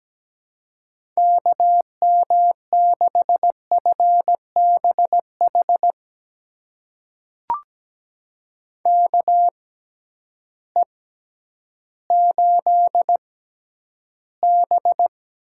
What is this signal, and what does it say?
Morse code, heard as a keyed tone, about 17 wpm, spaced as for 10 wpm, 700 Hz. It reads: KM6FBH KE8B
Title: USA Call Signs - 6 Characters - 6x Character Spacing - 17wpm